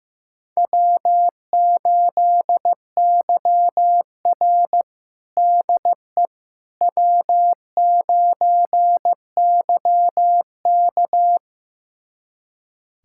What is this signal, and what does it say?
Morse code, heard as a keyed tone, about 16 wpm, 700 Hz. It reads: W8YR DE W9YK
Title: USA Callsign DE Callsign - 4 Characters - 15wpm